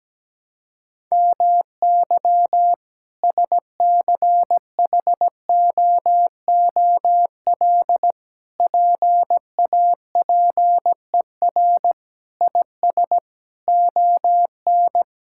Morse code, heard as a keyed tone, about 17 wpm, 700 Hz. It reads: MY SCHOOL PAPER IS ON
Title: Sentences from Top 1000 Words - Review Part 6 - 17wpm